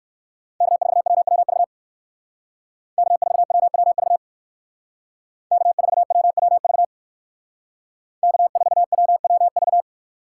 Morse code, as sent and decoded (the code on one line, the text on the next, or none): -.- ....- .-- .-- ...- / -.- ....- .-- .-- ...- / -.- ....- .-- .-- ...- / -.- ....- .-- .-- ...-
K4WWV K4WWV K4WWV K4WWV